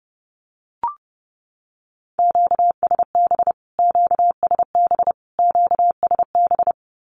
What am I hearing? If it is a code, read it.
Morse code, heard as a keyed tone, about 30 wpm, 700 Hz. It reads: QSB QSB QSB
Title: Q-Codes Repeated 3x 30wpm